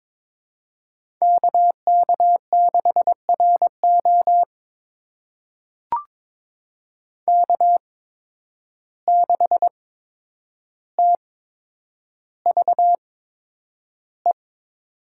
Morse code, heard as a keyed tone, about 22 wpm, spaced as for 9 wpm, 700 Hz. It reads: KK6RO K6TVE
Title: USA Call Signs - 5 Characters - 8x Character Spacing - 22wpm